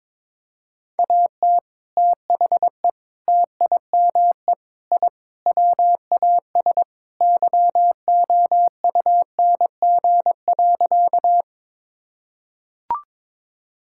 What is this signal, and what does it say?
Morse code, heard as a keyed tone, about 22 wpm, 700 Hz. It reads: AT THE TIME I WAS YOUNG.
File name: practice / 22wpm / Sentences with 3-Word Phrases - 22wpm